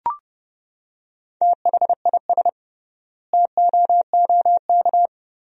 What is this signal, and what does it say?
Morse code, heard as a keyed tone, about 30 wpm, 700 Hz. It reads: THIS TOOK
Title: 3x Word-Spacing - Sentences from Top 300 Words - 30wpm